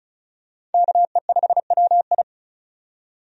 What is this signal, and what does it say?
Morse code, heard as a keyed tone, about 36 wpm, 700 Hz. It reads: KE5WI
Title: USA Call Signs - 5 Characters - 35wpm